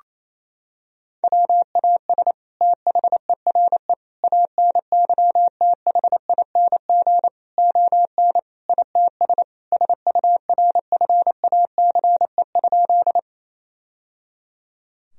Morse code, heard as a keyed tone, about 28 wpm, 700 Hz. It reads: WAS THERE ANYTHING ON ITS SURFACE?
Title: Sentences from Top 1000 Words - Review Part 3 - 28wpm